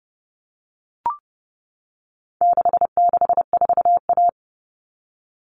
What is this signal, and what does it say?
Morse code, heard as a keyed tone, about 30 wpm, 700 Hz. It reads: 664A
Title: All Cut Numbers - 4-Digit 30wpm